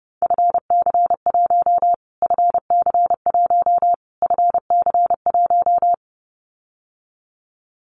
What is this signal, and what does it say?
Morse code, heard as a keyed tone, about 30 wpm, 700 Hz. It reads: FC1 FC1 FC1
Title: Callsign Prefixes for Top 25 DXCC - Repeated 3x - 30wpm